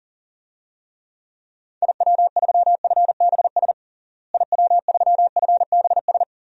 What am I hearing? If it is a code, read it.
Morse code, heard as a keyed tone, about 40 wpm, 700 Hz. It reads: IW3FBS IW3FBS
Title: DXCC-100 Call Signs - 6 Characters - Repeated 2x - 40wpm